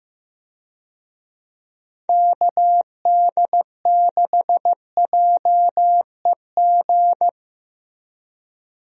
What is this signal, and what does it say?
Morse code, heard as a keyed tone, about 15 wpm, 700 Hz. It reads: KD6JEG
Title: USA Call Signs - 6 Characters - 15wpm